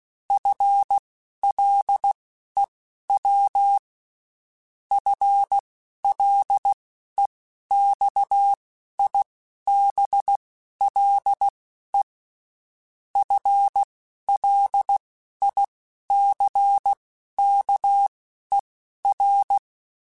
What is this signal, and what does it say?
Morse code, heard as a keyed tone, about 16 wpm, spaced as for 11 wpm, 800 Hz. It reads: FLEW FLEXIBLE FLICKER
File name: Spaced / FL Runs sp mp3s / FL Runs sp @15